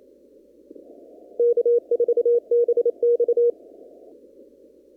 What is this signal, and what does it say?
Morse code, heard as a keyed tone, about 27 wpm, 480 Hz. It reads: K4BX
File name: CW / kt5j wpx cw-op-k5pi